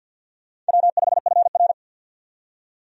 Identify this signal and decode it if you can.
Morse code, heard as a keyed tone, about 50 wpm, 700 Hz. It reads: W5FR
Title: USA Call Signs - 4 Characters - 50wpm